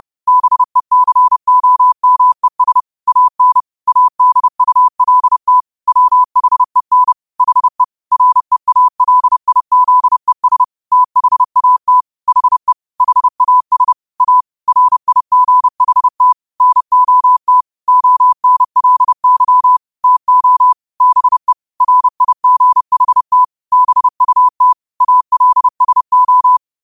Morse code, heard as a keyed tone, about 30 wpm, 1000 Hz. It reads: BECOMES AN ADULT WHEN HE REALIZES THAT HE HAS A RIGHT NOT ONLY TO BE RIGHT BUT ALSO
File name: qotdfiles / 30wpm